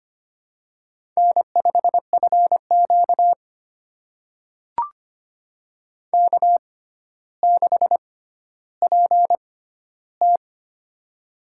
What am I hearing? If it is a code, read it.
Morse code, heard as a keyed tone, about 25 wpm, spaced as for 13 wpm, 700 Hz. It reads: N5FQ K6PT
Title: USA Call Signs - 4 Characters - 6x Character Spacing - 25wpm